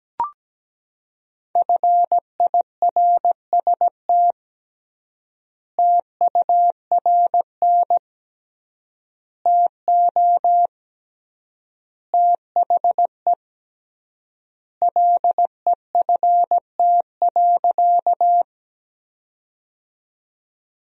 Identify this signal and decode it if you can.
Morse code, heard as a keyed tone, about 17 wpm, 700 Hz. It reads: FIRST TURN TO THE LEFT.
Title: 3x Word-Spacing - Sentences from Top 300 Words - 17wpm